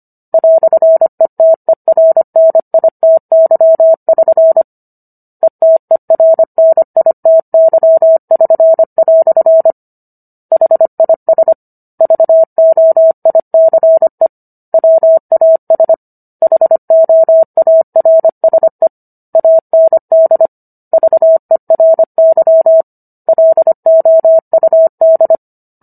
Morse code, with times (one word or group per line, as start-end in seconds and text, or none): 0.3-4.6 "ETERNITY<SN>
5.4-9.7 ETERNITY<SN>"
10.5-11.5 HIS
12.0-14.3 VOICE
14.7-15.9 WAS
16.4-18.9 HOARSE
19.4-20.5 AND
20.9-22.8 VERY
23.3-25.4 LOUD